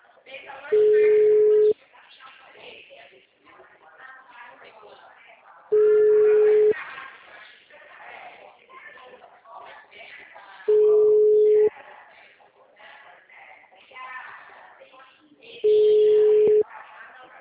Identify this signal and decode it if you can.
Ringback tone (Europe, 425 Hz), with speech.
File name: out- -2041- - - .3147